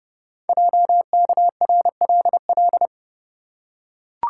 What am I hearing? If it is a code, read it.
Morse code, heard as a keyed tone, about 29 wpm, 690 Hz. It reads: JKRLL
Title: Code Groups - 5 Letters - 30wpm